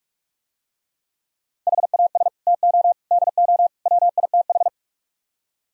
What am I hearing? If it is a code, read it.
Morse code, heard as a keyed tone, about 45 wpm, 700 Hz. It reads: HAS TO DO WITH